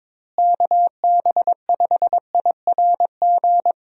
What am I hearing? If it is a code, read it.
Morse code, heard as a keyed tone, about 22 wpm, 700 Hz. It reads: KB5IRG
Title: USA Call Signs - 6 Characters - Repeated 3x - 22wpm